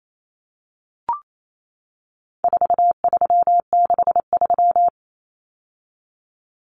Morse code, heard as a keyed tone, about 28 wpm, 700 Hz. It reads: 4363
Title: Common Cut Numbers - 4-Digit 28wpm